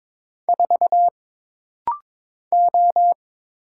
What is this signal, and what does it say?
Morse code, heard as a keyed tone, about 22 wpm, 700 Hz. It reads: 4 O
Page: This rapid-fire timing is geared to push your ICR (Instant Character Recognition) skills.